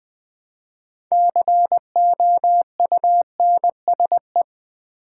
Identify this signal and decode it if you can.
Morse code, heard as a keyed tone, about 20 wpm, 700 Hz. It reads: COUNSE
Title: Build A Word - 3 Letters At A Time - 9-Letter Words - 20wpm